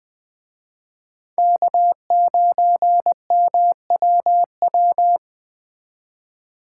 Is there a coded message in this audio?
Yes: Morse code from 1.4 to 5.2 s, heard as a keyed tone, about 19 wpm, 700 Hz. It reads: K9MWW